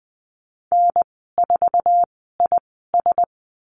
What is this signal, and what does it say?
Morse code, heard as a keyed tone, about 20 wpm, spaced as for 15 wpm, 700 Hz. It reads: N4IS
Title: 076 Lesson 13g - Letter L - Call Signs 20wpm